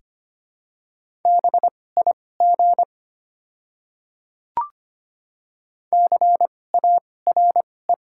Morse code, heard as a keyed tone, about 25 wpm, spaced as for 18 wpm, 700 Hz. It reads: BIG CARE
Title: 2x Character Spacing - Top 500 Words - 25wpm